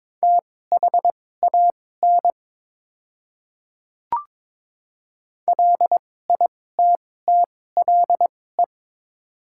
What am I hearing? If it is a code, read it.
Morse code, heard as a keyed tone, about 22 wpm, spaced as for 16 wpm, 690 Hz. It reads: THAN LITTLE
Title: 2x Character Spacing - Top 200 Words - 22wpm